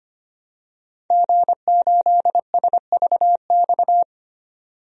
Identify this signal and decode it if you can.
Morse code, heard as a keyed tone, about 25 wpm, 700 Hz. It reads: G8SVX
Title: DXCC-100 Call Signs - 5 Characters - 25wpm